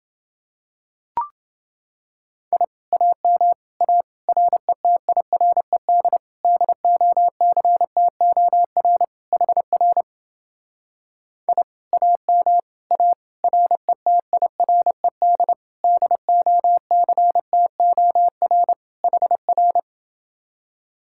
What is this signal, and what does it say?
Morse code, heard as a keyed tone, about 29 wpm, 700 Hz. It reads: I AM A RETIRED DOCTOR HR I AM A RETIRED DOCTOR HR